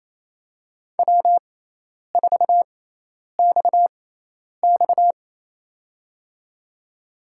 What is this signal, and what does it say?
Morse code, heard as a keyed tone, about 27 wpm, spaced as for 10 wpm, 700 Hz. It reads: W4XX
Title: USA Call Signs - 4 Characters - 6x Character Spacing - 28wpm